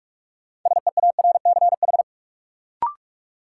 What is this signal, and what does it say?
Morse code, heard as a keyed tone, about 46 wpm, 700 Hz. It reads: SEARCH
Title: Sentences from Top 800 Words - New Word Drill - 45wpm